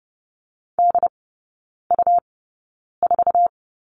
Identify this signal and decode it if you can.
Morse code, heard as a keyed tone, about 30 wpm, spaced as for 9 wpm, 700 Hz. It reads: DU4
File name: Lesson 17e - Letter u - Three Characters 30wpm